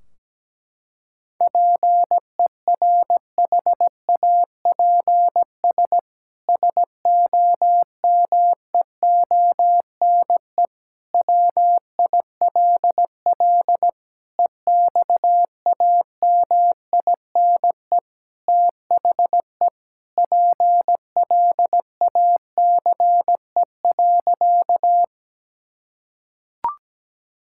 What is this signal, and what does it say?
Morse code, heard as a keyed tone, about 17 wpm, 700 Hz. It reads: PERHAPS SOMEONE WILL EXAMINE THE PLACE.
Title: Sentences from Top 1600 Words 17wpm